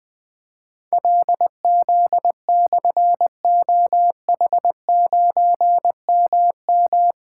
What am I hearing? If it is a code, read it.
Morse code, heard as a keyed tone, about 20 wpm, 700 Hz. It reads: LZ/OH9MM
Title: Difficult Call Signs 20wpm